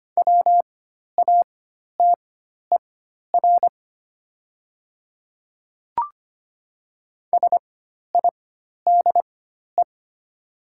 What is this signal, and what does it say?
Morse code, heard as a keyed tone, about 25 wpm, spaced as for 12 wpm, 700 Hz. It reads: WATER SIDE
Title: 4x Character Spacing - Top 100 Words - 25wpm